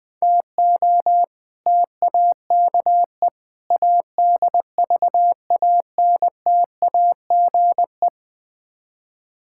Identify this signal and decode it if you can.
Morse code, heard as a keyed tone, about 20 wpm, 700 Hz. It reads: TO TAKE ADVANTAGE
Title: Build A Phrase - 5-Word Phrases - 20wpm